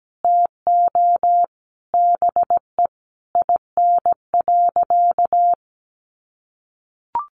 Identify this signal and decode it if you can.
Morse code, heard as a keyed tone, about 17 wpm, 700 Hz. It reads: TO BE IN.
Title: Sentences from Top 1200 Words 17wpm